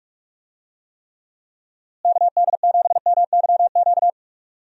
Morse code, heard as a keyed tone, about 46 wpm, 700 Hz. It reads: KD7MYQ
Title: USA Call Signs - 6 Characters - Repeated 3x - 45wpm